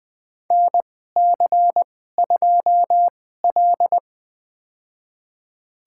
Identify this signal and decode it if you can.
Morse code, heard as a keyed tone, about 20 wpm, spaced as for 14 wpm, 700 Hz. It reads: NC2L
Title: USA Call Signs - 2x Character Spacing - 20wpm